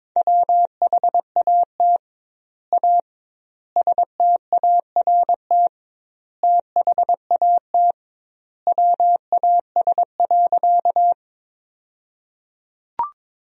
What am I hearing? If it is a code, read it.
Morse code, heard as a keyed tone, about 22 wpm, 700 Hz. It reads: WHAT A START THAT WAS.
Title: 2x Word-Spacing - Sentences from Top 300 Words - 22wpm